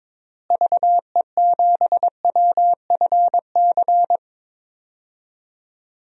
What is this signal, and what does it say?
Morse code, heard as a keyed tone, about 22 wpm, 700 Hz. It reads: VE7WFC
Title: DXCC-100 Call Signs - 6 Characters - Repeated 3x - 22wpm